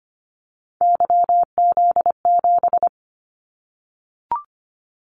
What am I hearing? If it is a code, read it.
Morse code, heard as a keyed tone, about 25 wpm, 700 Hz. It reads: YZ7
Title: Callsign Prefixes for Top 25 DXCC - 25wpm